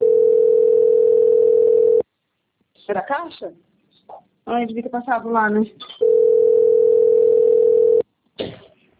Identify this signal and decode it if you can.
Ringback tone (North America), 440 + 480 Hz, with speech.
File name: out- - - - - .28403